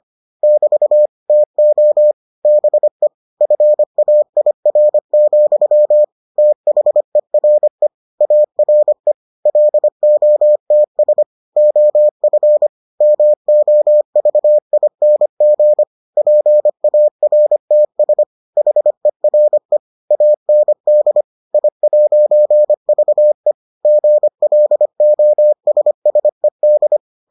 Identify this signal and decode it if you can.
Morse code, heard as a keyed tone, about 25 wpm, 600 Hz. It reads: <BT> TO BE FAIR, THERE ARE LOTS OF MOVING PARTS HERE AND I'VE GLOSSED